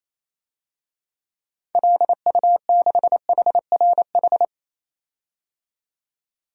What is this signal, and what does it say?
Morse code, heard as a keyed tone, about 27 wpm, 700 Hz. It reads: LU6HRH
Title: DXCC-100 Call Signs - 6 Characters - 28wpm